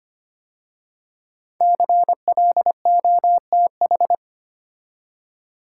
Morse code, heard as a keyed tone, about 25 wpm, 700 Hz. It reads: CLOTH
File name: 5 Letter Words - First 3 Letters Given - 25wpm